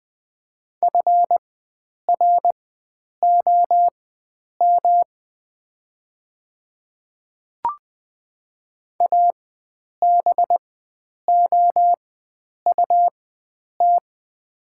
Morse code, heard as a keyed tone, about 20 wpm, spaced as for 9 wpm, 700 Hz. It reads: FROM ABOUT
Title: 4x Character Spacing - Top 100 Words - 20wpm